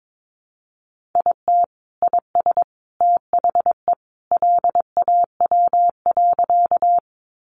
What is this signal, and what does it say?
Morse code, heard as a keyed tone, about 22 wpm, 700 Hz. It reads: IT IS THE LAW.
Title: Sentences from Top 600 Words 22wpm